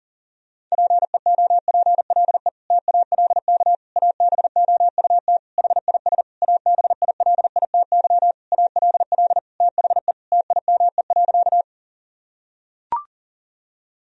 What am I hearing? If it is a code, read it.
Morse code, heard as a keyed tone, about 40 wpm, 700 Hz. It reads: PEOPLE TALK ABOUT HIS ABILITY ALL THE TIME.
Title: Sentences from Top 1000 Words - Review Part 3 - 40wpm